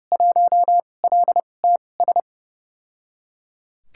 Morse code, heard as a keyed tone, about 31 wpm, spaced as for 22 wpm, 700 Hz. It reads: 1LTS
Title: USA Call Signs - 2x Character Spacing - 30wpm